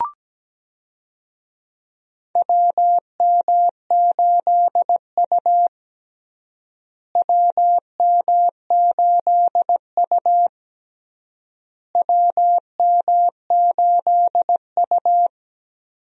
Morse code, heard as a keyed tone, about 17 wpm, 700 Hz. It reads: WM8U WM8U WM8U
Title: USA Call Signs - 4 Characters - Repeated 3x - 17wpm